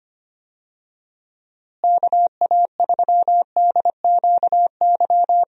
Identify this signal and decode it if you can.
Morse code, heard as a keyed tone, about 25 wpm, 700 Hz. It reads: KA3DQY
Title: USA Call Signs - 6 Characters - Repeated 3x - 25wpm